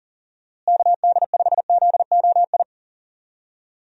Morse code, heard as a keyed tone, about 37 wpm, 700 Hz. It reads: KD5ZOI
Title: USA Call Signs - 6 Characters - Repeated 3x - 40wpm